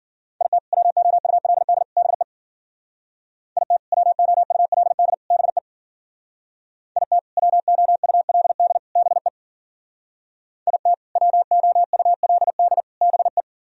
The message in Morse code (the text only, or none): IT WOULD BE IT WOULD BE IT WOULD BE IT WOULD BE